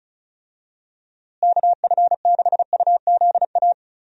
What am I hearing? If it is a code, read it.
Morse code, heard as a keyed tone, about 36 wpm, 700 Hz. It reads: KF6UZA
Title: USA Call Signs - 6 Characters - Repeated 3x - 35wpm